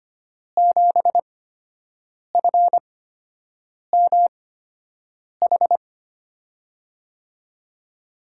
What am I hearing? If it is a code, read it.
Morse code, heard as a keyed tone, about 25 wpm, spaced as for 7 wpm, 700 Hz. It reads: 7FMH
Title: USA Call Signs - 5 Characters - 8x Character Spacing - 25wpm